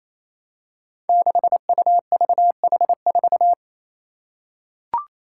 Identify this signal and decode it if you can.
Morse code, heard as a keyed tone, about 28 wpm, 700 Hz. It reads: 6UVH4
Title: Five Challenging Letter-Number Combinations 28wpm